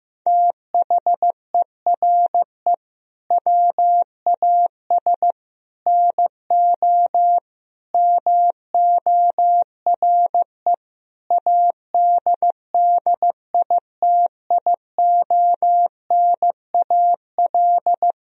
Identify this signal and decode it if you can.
Morse code, heard as a keyed tone, about 15 wpm, 700 Hz. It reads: THERE WAS NO MORE ADDITIONAL
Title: Sentences from Top 1000 Words - Review Part 4 - 15wpm